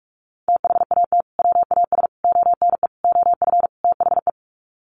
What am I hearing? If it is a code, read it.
Morse code, heard as a keyed tone, about 44 wpm, 700 Hz. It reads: THAT WAS ONE OF THE
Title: Sets of 5 Words 45wpm